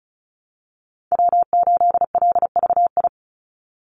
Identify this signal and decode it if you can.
Morse code, heard as a keyed tone, about 34 wpm, 700 Hz. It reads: W8LVI
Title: Call Signs Repeated 2x - 35wpm